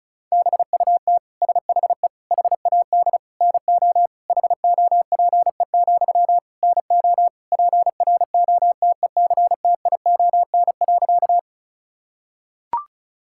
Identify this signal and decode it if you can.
Morse code, heard as a keyed tone, about 36 wpm, 700 Hz. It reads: BUT SHE HAD NO HOPE, NO PROTECTION.
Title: Sentences from Top 1600 Words 35wpm